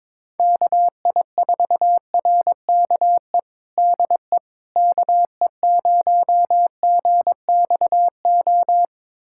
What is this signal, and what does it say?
Morse code, heard as a keyed tone, about 22 wpm, 700 Hz. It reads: KI4RKE DE KE0GXO
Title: USA Callsign DE Callsign - 6 Characters - 22wpm